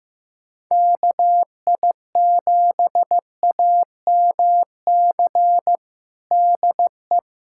Morse code, heard as a keyed tone, about 15 wpm, 700 Hz. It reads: KI7AMC DE
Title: USA Callsign DE Callsign 15wpm